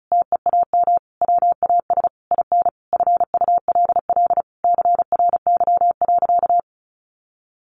Morse code, heard as a keyed tone, about 34 wpm, 700 Hz. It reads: TEAM WAS IN FULL CRY.
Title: Sentences from Top 1500 Words 35wpm